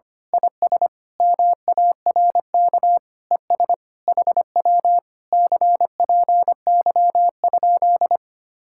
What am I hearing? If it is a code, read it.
Morse code, heard as a keyed tone, about 24 wpm, 700 Hz. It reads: IS MARK ES HW CPY?
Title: Speed-Racing Common QSO Phrases - 25wpm